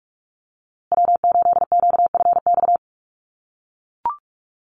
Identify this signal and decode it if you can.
Morse code, heard as a keyed tone, about 44 wpm, 690 Hz. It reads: W8QFX
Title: USA Call Signs Repeated 2x 45wpm